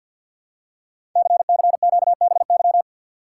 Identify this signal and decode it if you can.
Morse code, heard as a keyed tone, about 48 wpm, 700 Hz. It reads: CCQBY
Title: Code Groups - 5 Letters - 50wpm